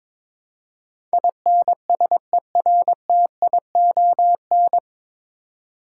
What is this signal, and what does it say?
Morse code, heard as a keyed tone, about 22 wpm, 700 Hz. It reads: INSERTION
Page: Morse Code Ninja practice 22wpm